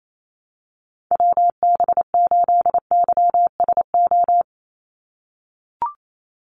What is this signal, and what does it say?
Morse code, heard as a keyed tone, about 28 wpm, 700 Hz. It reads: WB8YSO
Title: USA Call Signs 28wpm